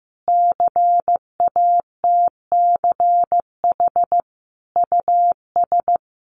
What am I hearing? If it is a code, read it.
Morse code, heard as a keyed tone, about 14 wpm, 700 Hz. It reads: CATCH US
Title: Sentences from Top 1500 Words 15wpm